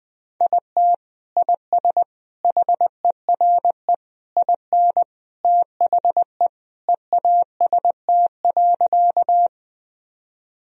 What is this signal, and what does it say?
Morse code, heard as a keyed tone, about 20 wpm, 700 Hz. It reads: IT IS HERE IN THE EAST.
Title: Sentences from Top 1000 Words - Review Part 3 - 20wpm